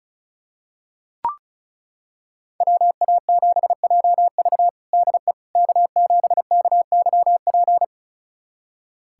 Morse code, heard as a keyed tone, about 34 wpm, 700 Hz. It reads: WA7JV DE K7KYP
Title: USA Callsign DE Callsign - 5 Characters - 35wpm